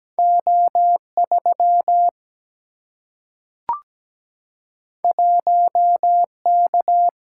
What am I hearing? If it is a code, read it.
Morse code, heard as a keyed tone, about 17 wpm, 700 Hz. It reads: O 3 1 K
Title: Single-Letter Single-Number 17wpm